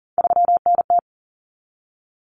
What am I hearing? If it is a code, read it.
Morse code, heard as a keyed tone, about 40 wpm, 700 Hz. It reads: 3NT
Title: Common Cut Numbers - 3-Digit 40wpm